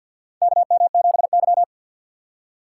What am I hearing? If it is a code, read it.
Morse code, heard as a keyed tone, about 47 wpm, 700 Hz. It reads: KM7Y
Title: USA Call Signs - 4 Characters - 2x Character Spacing - 50wpm